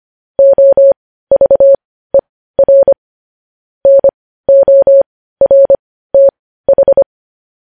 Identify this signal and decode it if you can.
Morse code, heard as a keyed tone, about 25 wpm, spaced as for 15 wpm, 550 Hz. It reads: OVER NORTH
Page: news headlines in morse code at 15 wpm